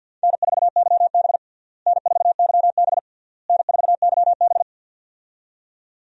Morse code, heard as a keyed tone, about 49 wpm, 700 Hz. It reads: N4YB N4YB N4YB
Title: USA Call Signs - 4 Characters - Repeated 3x - 50wpm